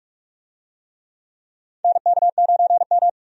Morse code, heard as a keyed tone, about 43 wpm, 700 Hz. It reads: NK9M
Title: USA Call Signs - 4 Characters - 45wpm